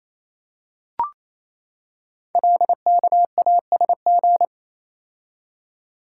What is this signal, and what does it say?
Morse code, heard as a keyed tone, about 28 wpm, 700 Hz. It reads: LKASG
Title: Code Groups - 5 Letters - 28wpm